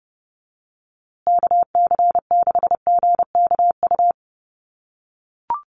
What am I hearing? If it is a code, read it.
Morse code, heard as a keyed tone, about 30 wpm, 700 Hz. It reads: KC6GKU
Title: USA Call Signs Repeated 3x 30wpm